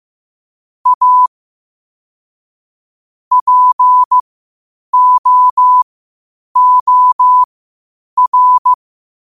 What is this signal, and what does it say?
Morse code, heard as a keyed tone, about 15 wpm, spaced as for 8 wpm, 1000 Hz. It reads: A POOR
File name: qotdfiles / 8wpm